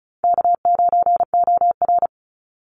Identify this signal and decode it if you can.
Morse code, heard as a keyed tone, about 34 wpm, 700 Hz. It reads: K9OR
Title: Easy Call Signs Repeated 3x - 35wpm